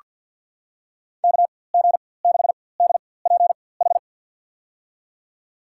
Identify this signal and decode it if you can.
Morse code, heard as a keyed tone, about 48 wpm, spaced as for 23 wpm, 700 Hz. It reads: KG6DPH
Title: USA Call Signs - 6 Characters - 4x Character Spacing - 50wpm